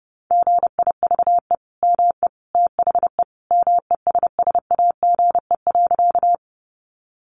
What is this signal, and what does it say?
Morse code, heard as a keyed tone, about 30 wpm, 700 Hz. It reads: GIVE ME THE MESSAGE.
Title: Sentences from Top 1100 Words 30wpm